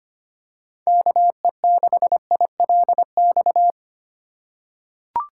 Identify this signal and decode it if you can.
Morse code, heard as a keyed tone, about 25 wpm, 700 Hz. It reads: KE6ILX
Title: USA Call Signs - 6 Characters - 4x Character Spacing - 25wpm